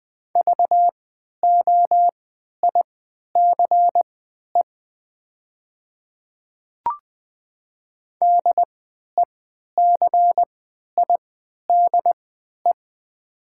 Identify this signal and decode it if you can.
Morse code, heard as a keyed tone, about 20 wpm, spaced as for 11 wpm, 700 Hz. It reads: VOICE DECIDE
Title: 3x Character Spacing - Top 500 Words - 20wpm